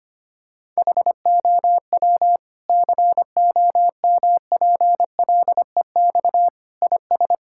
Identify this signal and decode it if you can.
Morse code, heard as a keyed tone, about 25 wpm, 700 Hz. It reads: HOW COMPLEX IS
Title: Sentences from Top 1000 Words - Review Part 6 - 25wpm